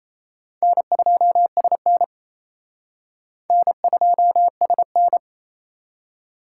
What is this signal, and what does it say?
Morse code, heard as a keyed tone, about 34 wpm, 700 Hz. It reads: N2SN N2SN
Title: Speed Racing - USA 1x2 Call Signs 28wpm